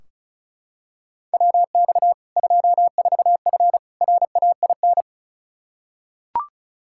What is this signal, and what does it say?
Morse code, heard as a keyed tone, about 36 wpm, 700 Hz. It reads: WX 24F RAIN
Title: Practice QSO - Exchange 2 - Info - 35wpm